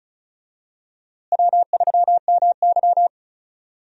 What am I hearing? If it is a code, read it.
Morse code, heard as a keyed tone, about 36 wpm, 700 Hz. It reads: W3MY
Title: USA Call Signs - 4 Characters - Repeated 3x - 35wpm